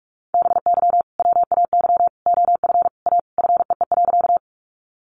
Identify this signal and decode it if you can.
Morse code, heard as a keyed tone, about 44 wpm, 700 Hz. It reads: BY WAY OF A FEE.
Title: Sentences from Top 1500 Words 45wpm